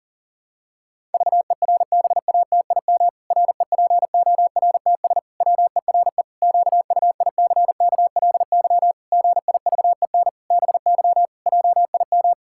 Morse code, heard as a keyed tone, about 37 wpm, 700 Hz. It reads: VERBATIM REPORTS WERE QUICKLY GIVEN BY JIM
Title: Pangram Challenge v2 - 40wpm